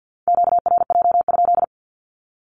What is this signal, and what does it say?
Morse code, heard as a keyed tone, about 48 wpm, 700 Hz. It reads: QRJ?
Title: Q-Codes 50wpm